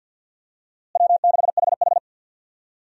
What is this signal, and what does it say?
Morse code, heard as a keyed tone, about 48 wpm, 700 Hz. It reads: W6HH
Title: USA Call Signs - 4 Characters - Repeated 2x - 50wpm